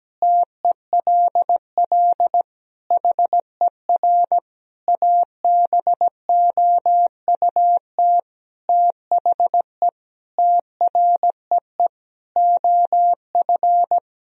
Morse code, heard as a keyed tone, about 17 wpm, 700 Hz. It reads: TELL HER ABOUT THE TREE OF
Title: Sentences from Top 1000 Words - Review Part 1 - 17wpm